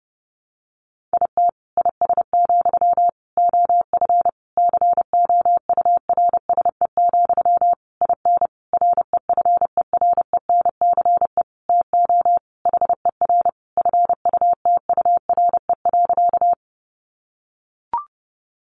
Morse code, heard as a keyed tone, about 30 wpm, 700 Hz. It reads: IT IS, OF COURSE, IN REFERENCE TO HER FUTURE.
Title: Sentences from Top 1200 Words 30wpm